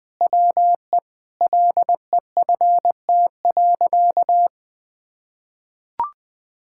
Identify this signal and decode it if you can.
Morse code, heard as a keyed tone, about 21 wpm, 700 Hz. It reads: WE LEFT.
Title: Sentences from Top 1000 Words - Review Part 6 - 20wpm